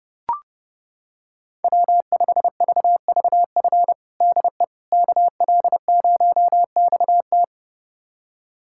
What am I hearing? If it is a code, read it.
Morse code, heard as a keyed tone, about 30 wpm, 700 Hz. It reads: W5VVF DE KL0XT
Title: USA Callsign DE Callsign - 5 Characters - 30wpm